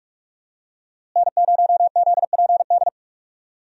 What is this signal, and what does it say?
Morse code, heard as a keyed tone, about 46 wpm, 700 Hz. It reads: N0ZPD
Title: USA Call Signs - 5 Characters - Repeated 3x - 45wpm